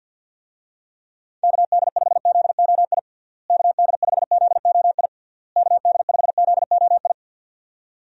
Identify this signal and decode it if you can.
Morse code, heard as a keyed tone, about 53 wpm, 700 Hz. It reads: KD5ZOI KD5ZOI KD5ZOI
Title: USA Call Signs - 6 Characters - Repeated 3x - 50wpm